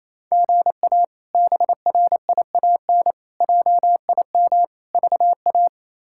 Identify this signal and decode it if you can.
Morse code, heard as a keyed tone, about 28 wpm, 700 Hz. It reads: GA BRIAN JIM VA
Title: K1USN SST QSO Practice - 28wpm